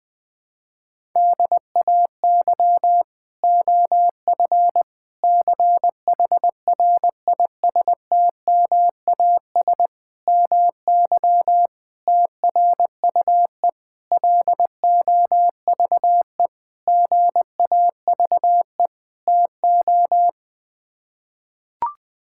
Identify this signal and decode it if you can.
Morse code, heard as a keyed tone, about 20 wpm, 700 Hz. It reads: DAY OF CHRISTMAS MY TRUE LOVE GAVE TO
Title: Sets of 8 Words - 20wpm